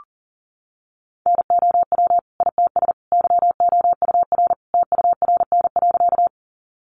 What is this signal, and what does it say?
Morse code, heard as a keyed tone, about 37 wpm, 700 Hz. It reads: NOW ITS YOUR TURN.
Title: Sentences from Top 600 Words 40wpm